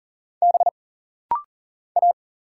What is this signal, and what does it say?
Morse code, heard as a keyed tone, about 40 wpm, 700 Hz. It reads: B A
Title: Single Letters - Rapid-Fire 40wpm